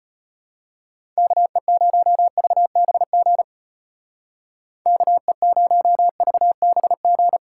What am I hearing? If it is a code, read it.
Morse code, heard as a keyed tone, about 31 wpm, 700 Hz. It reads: KE0VBG KE0VBG
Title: Speed Racing - USA Call Signs 25wpm